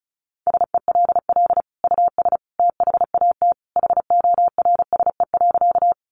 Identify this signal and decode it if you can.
Morse code, heard as a keyed tone, about 34 wpm, 700 Hz. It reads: SELL US THAT HORSE.
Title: Sentences from Top 1100 Words 35wpm